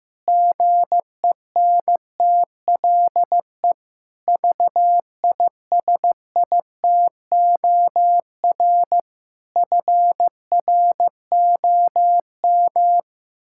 Morse code, heard as a keyed tone, about 15 wpm, 700 Hz. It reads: GENTLE VISITOR FROM
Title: Pangram Challenge v2 - 15wpm